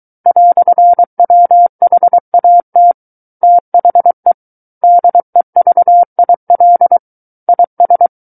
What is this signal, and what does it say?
Morse code, heard as a keyed tone, about 23 wpm, 700 Hz. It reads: "WHAT THE DEVIL IS
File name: Book One / 14-Chapter 14 - 23 WPM